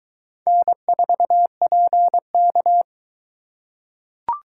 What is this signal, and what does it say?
Morse code, heard as a keyed tone, about 22 wpm, 700 Hz. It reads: N4PK